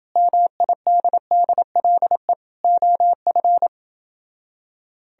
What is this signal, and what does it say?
Morse code, heard as a keyed tone, about 29 wpm, 700 Hz. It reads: MIDDLE OF